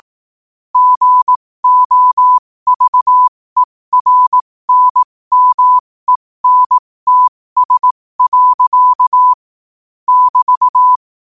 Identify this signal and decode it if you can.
Morse code, heard as a keyed tone, about 18 wpm, spaced as for 15 wpm, 1000 Hz. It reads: GOVERNMENTS. <BT>
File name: qotdfiles / 15wpm